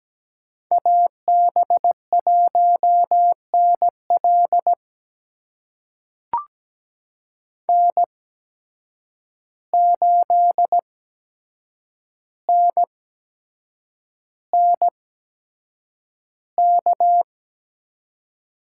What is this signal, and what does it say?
Morse code, heard as a keyed tone, about 17 wpm, spaced as for 7 wpm, 700 Hz. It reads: AB1NL N8NNK
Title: USA Call Signs - 5 Characters - 8x Character Spacing - 17wpm